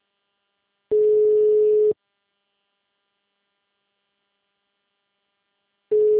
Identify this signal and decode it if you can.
Ringback tone (Europe, 425 Hz).